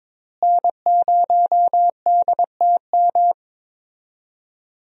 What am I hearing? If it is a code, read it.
Morse code, heard as a keyed tone, about 22 wpm, 700 Hz. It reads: N0DTM
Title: USA Call Signs - 5 Characters - Repeated 3x - 22wpm